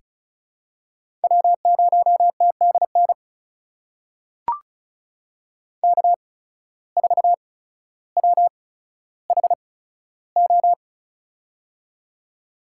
Morse code, heard as a keyed tone, about 36 wpm, spaced as for 15 wpm, 700 Hz. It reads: W0TDN K4WHO
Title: USA Call Signs - 5 Characters - 8x Character Spacing - 35wpm